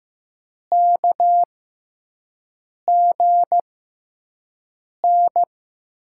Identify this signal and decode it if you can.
Morse code, heard as a keyed tone, about 15 wpm, spaced as for 5 wpm, 700 Hz. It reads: KGN
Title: USA Call Signs - 6 Characters - 6x Character Spacing - 15wpm